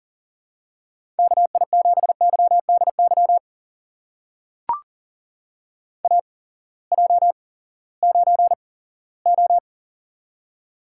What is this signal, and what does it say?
Morse code, heard as a keyed tone, about 40 wpm, spaced as for 20 wpm, 700 Hz. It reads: KI7YDY AJ9O
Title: USA Call Signs - 8x Character Spacing - 40wpm